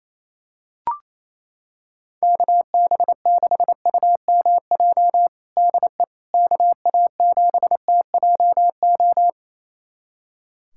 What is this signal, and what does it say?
Morse code, heard as a keyed tone, about 28 wpm, 700 Hz. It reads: KB6UMJ DE KA7TJO
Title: USA Callsign DE Callsign - 6 Characters - 28wpm